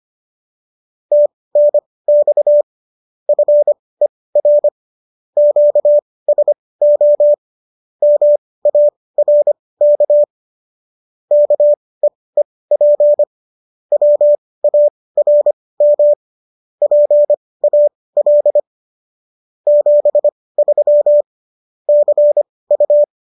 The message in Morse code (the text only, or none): TNX FER QSO MARK KEEP WARM PAL 73 CU